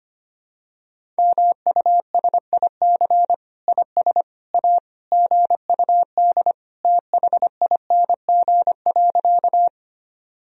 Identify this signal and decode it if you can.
Morse code, heard as a keyed tone, about 25 wpm, 700 Hz. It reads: MUSIC IS A GUD THING.
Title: Sentences from Top 300 Words and Abbreviations 25wpm